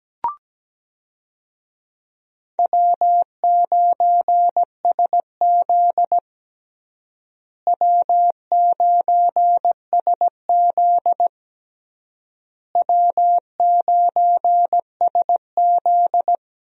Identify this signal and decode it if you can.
Morse code, heard as a keyed tone, about 17 wpm, 700 Hz. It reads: W9SZ W9SZ W9SZ
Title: USA Call Signs - 4 Characters - Repeated 3x - 17wpm